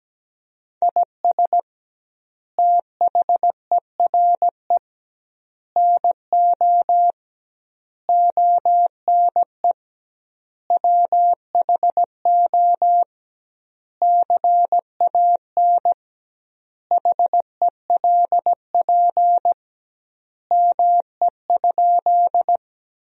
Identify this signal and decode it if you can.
Morse code, heard as a keyed tone, about 17 wpm, 700 Hz. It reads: IS THERE NO ONE WHO CAN HELP ME?
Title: 2x Word-Spacing - Sentences from Top 200 Words - 17wpm